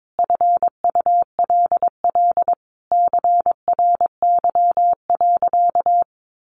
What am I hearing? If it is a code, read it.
Morse code, heard as a keyed tone, about 22 wpm, 700 Hz. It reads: FULL CRY.
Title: Sentences from Top 1500 Words 22wpm